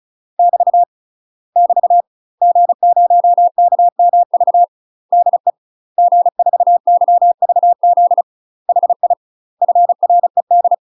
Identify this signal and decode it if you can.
Morse code, heard as a keyed tone, about 34 wpm, 700 Hz. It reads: <BT> <BT> G0KMV DE G4YVZ HI FRED
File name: UK MASTER QSOs TRAINING/35WPM